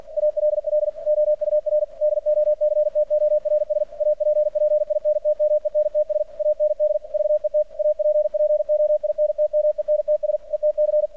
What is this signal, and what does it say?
Morse code, heard as a keyed tone, about 48 wpm, 600 Hz. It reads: ALL OUR DOCTORS APPINTMENTS AND VET APPOINTMENTS ETC
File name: archived / CRSnet